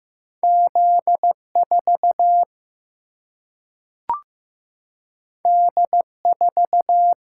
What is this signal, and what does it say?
Morse code, heard as a keyed tone, about 15 wpm, 700 Hz. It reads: Z4 D4
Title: Single-Letter Single-Number 15wpm